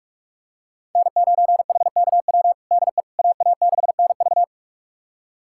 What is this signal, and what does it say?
Morse code, heard as a keyed tone, about 46 wpm, 700 Hz. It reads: N9HKW DE AA6NV
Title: USA Callsign DE Callsign - 5 Characters - 45wpm